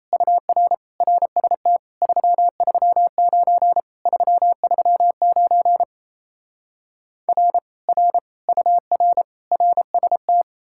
Morse code, heard as a keyed tone, about 38 wpm, 700 Hz. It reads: UR RST 339 339 R R UR RST